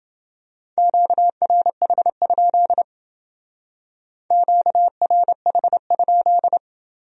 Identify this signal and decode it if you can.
Morse code, heard as a keyed tone, about 29 wpm, 700 Hz. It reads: QRH? QRH?